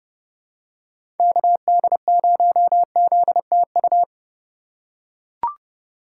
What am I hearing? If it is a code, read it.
Morse code, heard as a keyed tone, about 31 wpm, 700 Hz. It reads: KD0ZTU
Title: USA Call Signs - 6 Characters - 2x Character Spacing - 30wpm